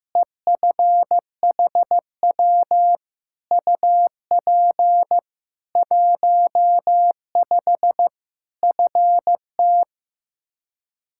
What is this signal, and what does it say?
Morse code, heard as a keyed tone, about 15 wpm, 700 Hz. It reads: EFHW UP 15 FT
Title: K1USN SST QSO Practice 15wpm